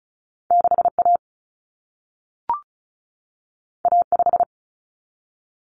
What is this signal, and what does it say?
Morse code, heard as a keyed tone, about 36 wpm, 700 Hz. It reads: 6A A5
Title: Common Cut Numbers - 2-Digit 35wpm